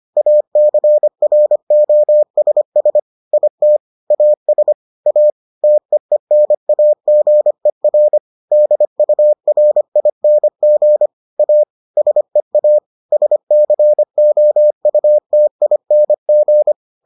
Morse code, heard as a keyed tone, about 25 wpm, 600 Hz. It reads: ACROSS IT AS A TEENAGER DURING A SEA SCOUTING